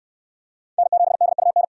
Morse code, heard as a keyed tone, about 62 wpm, 700 Hz. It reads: N7DUN